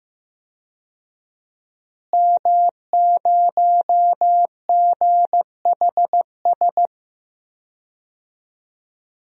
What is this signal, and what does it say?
Morse code, heard as a keyed tone, about 15 wpm, 700 Hz. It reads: M0GHS
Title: DXCC-100 Call Signs - 5 Characters - Repeated 2x - 15wpm